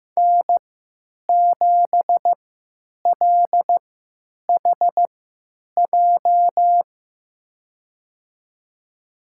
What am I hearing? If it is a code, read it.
Morse code, heard as a keyed tone, about 15 wpm, spaced as for 9 wpm, 700 Hz. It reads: N7LHJ
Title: USA Call Signs - 5 Characters - 3x Character Spacing - 15wpm